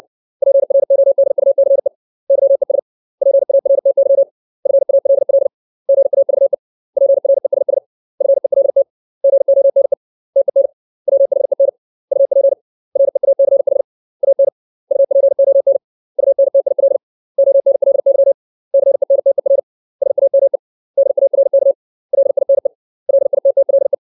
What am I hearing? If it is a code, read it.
Morse code, heard as a keyed tone, about 60 wpm, 550 Hz. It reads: PRODUCE YES PARTY FALL GAVE PRESS FELT GONE TEN WHEN UP REACH AN UPON UNTIL ONLY CENTER SAME DARK LINE LITTLE
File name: 632 words / 632 Words 60WPM